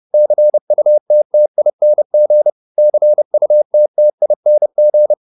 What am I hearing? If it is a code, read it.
Morse code, heard as a keyed tone, about 30 wpm, 600 Hz. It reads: CUTTING CUTTING